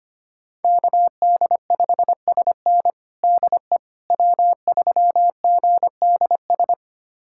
Morse code, heard as a keyed tone, about 25 wpm, 700 Hz. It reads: KD5SN DE W3GDS
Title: USA Callsign DE Callsign 25wpm